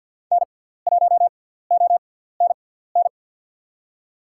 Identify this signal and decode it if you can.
Morse code, heard as a keyed tone, about 48 wpm, spaced as for 17 wpm, 690 Hz. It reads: N1ONN
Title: USA Call Signs - 6x Character Spacing - 50wpm